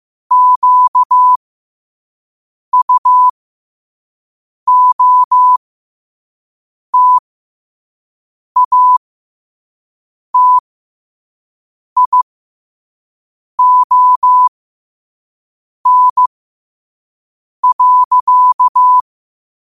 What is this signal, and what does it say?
Morse code, heard as a keyed tone, about 15 wpm, spaced as for 5 wpm, 1000 Hz. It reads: QUOTATION.
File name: qotdfiles / 5wpm